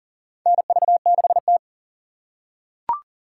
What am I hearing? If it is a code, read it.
Morse code, heard as a keyed tone, about 40 wpm, 700 Hz. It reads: NV6T
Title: USA Call Signs - 2x Character Spacing - 40wpm